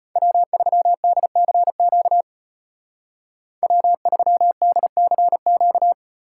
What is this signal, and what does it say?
Morse code, heard as a keyed tone, about 33 wpm, 700 Hz. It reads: W3DCQ W3DCQ